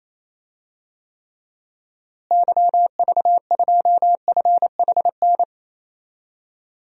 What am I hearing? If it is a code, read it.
Morse code, heard as a keyed tone, about 28 wpm, 700 Hz. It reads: YV2FHN
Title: DXCC-100 Call Signs - 6 Characters - Repeated 3x - 28wpm